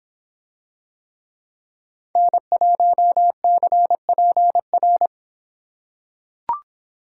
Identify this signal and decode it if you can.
Morse code, heard as a keyed tone, about 24 wpm, 700 Hz. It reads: N1CPR